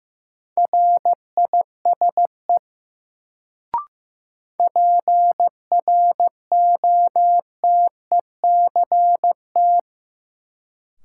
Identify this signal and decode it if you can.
Morse code, heard as a keyed tone, about 15 wpm, 700 Hz. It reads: RISE PROTECT
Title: Sentences from Top 900 Words - New Word Drill - 15wpm